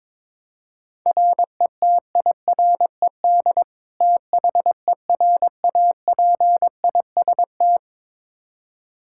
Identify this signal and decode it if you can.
Morse code, heard as a keyed tone, about 21 wpm, 700 Hz. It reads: RETIRED THERAPIST
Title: Practice QSO - Exchange 3 - More Info - 22wpm